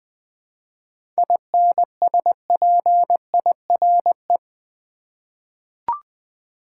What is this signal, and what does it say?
Morse code, heard as a keyed tone, about 20 wpm, 700 Hz. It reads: INSPIRE
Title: 7 Letter Words - First 3 Letters Given - 20wpm